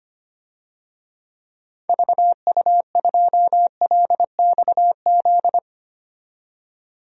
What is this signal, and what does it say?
Morse code, heard as a keyed tone, about 25 wpm, 700 Hz. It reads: VU2LXZ
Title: DXCC-100 Call Signs - 6 Characters - 25wpm